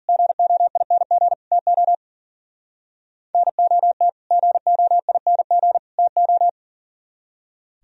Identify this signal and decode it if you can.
Morse code, heard as a keyed tone, about 51 wpm, 700 Hz. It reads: GOING TO NOT GOING TO